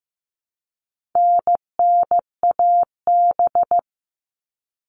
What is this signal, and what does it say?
Morse code, heard as a keyed tone, about 15 wpm, 700 Hz. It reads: NNAB
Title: All Cut Numbers - 4-Digit 15wpm